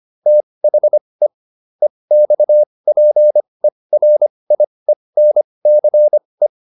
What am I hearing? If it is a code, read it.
Morse code, heard as a keyed tone, about 25 wpm, spaced as for 20 wpm, 600 Hz. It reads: THE EXPERIENCE
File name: SS206 20